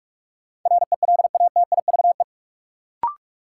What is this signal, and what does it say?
Morse code, heard as a keyed tone, about 45 wpm, 700 Hz. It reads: RELATIVE
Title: Sentences from Top 1400 Words - New Word Drill - 45wpm